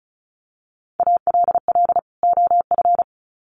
Morse code, heard as a keyed tone, about 34 wpm, 700 Hz. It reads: ALL OF
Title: Sets of 2 Words 35wpm